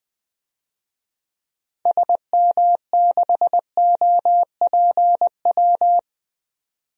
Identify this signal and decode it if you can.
Morse code, heard as a keyed tone, about 20 wpm, 700 Hz. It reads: SM6OPW
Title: DXCC-100 Call Signs - 6 Characters - 20wpm